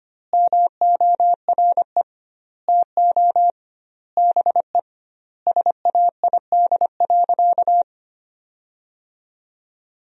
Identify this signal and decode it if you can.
Morse code, heard as a keyed tone, about 25 wpm, 700 Hz. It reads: MORE TO BE SAID.
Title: 2x Word-Spacing - Sentences from Top 500 Words - 25wpm